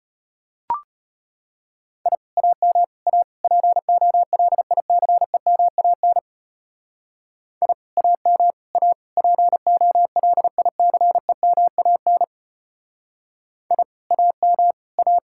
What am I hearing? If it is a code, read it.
Morse code, heard as a keyed tone, about 31 wpm, 700 Hz. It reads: I AM A POLICEMAN I AM A POLICEMAN I AM A
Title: Speed-Racing Common QSO Phrases Encore - 25wpm